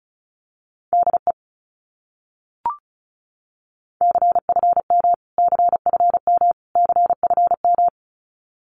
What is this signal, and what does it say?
Morse code, heard as a keyed tone, about 34 wpm, 700 Hz. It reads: DE CFM CFM CFM
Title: Abbreviations Repeated 3x - 35wpm